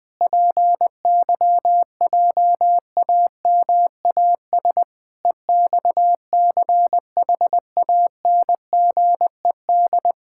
Morse code, heard as a keyed tone, about 20 wpm, 700 Hz. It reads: PYJAMAS EXCHANGED
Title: Pangram Challenge v2 - 20wpm